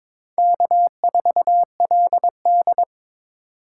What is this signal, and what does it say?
Morse code, heard as a keyed tone, about 22 wpm, 700 Hz. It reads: K4LD
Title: USA Call Signs - 4 Characters - 3x Character Spacing - 22wpm